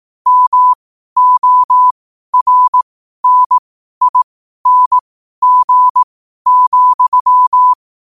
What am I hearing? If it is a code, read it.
Morse code, heard as a keyed tone, about 18 wpm, spaced as for 12 wpm, 1000 Hz. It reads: MORNING,
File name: qotdfiles / 12wpm